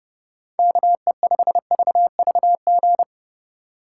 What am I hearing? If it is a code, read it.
Morse code, heard as a keyed tone, about 31 wpm, 700 Hz. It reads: KE5VVG